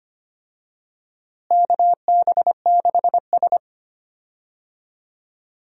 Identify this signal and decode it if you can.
Morse code, heard as a keyed tone, about 25 wpm, 700 Hz. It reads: KB6S
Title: USA Call Signs - 4 Characters - Repeated 3x - 25wpm